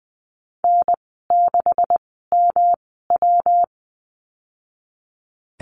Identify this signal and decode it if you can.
Morse code, heard as a keyed tone, about 20 wpm, spaced as for 15 wpm, 700 Hz. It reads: N6MW
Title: 173 Lesson 28g - Number 9 - Call Signs 20wpm